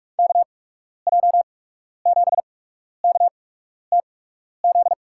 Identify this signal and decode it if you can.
Morse code, heard as a keyed tone, about 44 wpm, spaced as for 12 wpm, 700 Hz. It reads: KJ7KTZ
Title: USA Call Signs - 6 Characters - 8x Character Spacing - 45wpm